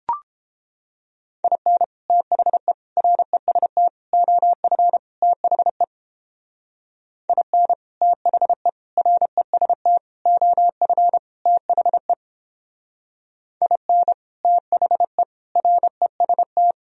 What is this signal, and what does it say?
Morse code, heard as a keyed tone, about 29 wpm, 700 Hz. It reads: IN THE REST OF THE IN THE REST OF THE IN THE REST